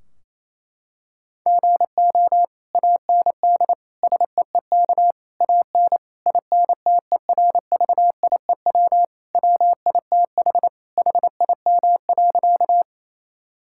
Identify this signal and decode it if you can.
Morse code, heard as a keyed tone, about 28 wpm, 700 Hz. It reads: GO AND SEEK AN INTERVIEW WITH HIM.
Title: Sentences from Top 1600 Words 28wpm